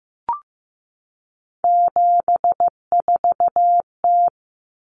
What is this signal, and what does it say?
Morse code, heard as a keyed tone, about 15 wpm, 700 Hz. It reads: 74T
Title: Common Cut Numbers - 3-Digit 15wpm